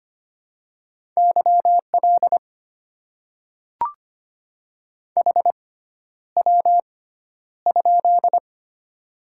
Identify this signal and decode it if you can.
Morse code, heard as a keyed tone, about 25 wpm, spaced as for 11 wpm, 690 Hz. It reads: YL HW?
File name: Top 90 QSO Elements - 6x Character Spacing - 25wpm